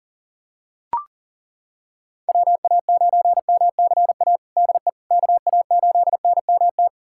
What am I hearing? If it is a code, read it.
Morse code, heard as a keyed tone, about 40 wpm, 700 Hz. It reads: WA9MCA DE KA8NMT
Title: USA Callsign DE Callsign - 6 Characters - 40wpm